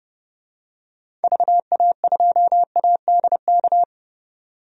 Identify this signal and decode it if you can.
Morse code, heard as a keyed tone, about 30 wpm, 700 Hz. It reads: VA2ADK
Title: DXCC-100 Call Signs - 6 Characters - 30wpm